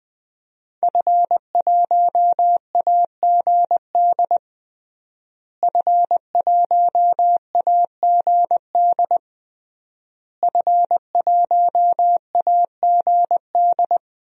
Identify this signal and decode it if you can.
Morse code, heard as a keyed tone, about 20 wpm, 700 Hz. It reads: F1AGD F1AGD F1AGD
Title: DXCC-100 Call Signs - 5 Characters - Repeated 3x - 20wpm